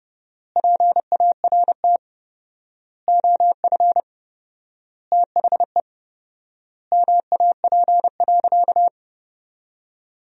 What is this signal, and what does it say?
Morse code, heard as a keyed tone, about 30 wpm, 700 Hz. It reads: PART OF THE MAP.
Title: 4x Word-Spacing - Sentences from Top 400 Words - 30wpm